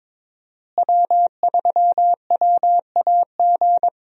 Morse code, heard as a keyed tone, about 22 wpm, 700 Hz. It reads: W3WAG
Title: USA Call Signs - 5 Characters - Repeated 2x - 22wpm